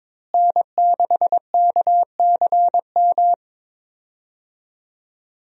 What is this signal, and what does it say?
Morse code, heard as a keyed tone, about 22 wpm, 700 Hz. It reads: N6KCM
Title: USA Call Signs - 5 Characters - Repeated 3x - 22wpm